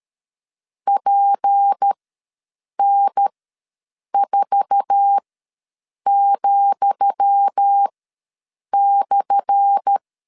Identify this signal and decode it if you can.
Morse code, heard as a keyed tone, about 13 wpm, spaced as for 7 wpm, 800 Hz. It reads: PN4,/
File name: LEZIONI / 10 SIGNS / 10D COMA ws 12wpm 3618